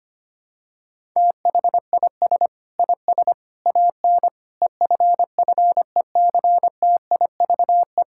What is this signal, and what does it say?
Morse code, heard as a keyed tone, about 25 wpm, 700 Hz. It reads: THIS IS AN EFFECTIVE
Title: Sentences from Top 1000 Words - Review Part 6 - 25wpm